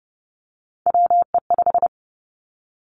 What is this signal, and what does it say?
Morse code, heard as a keyed tone, about 30 wpm, 700 Hz. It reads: WE5
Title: Three Challenging Letter-Number Combinations 30wpm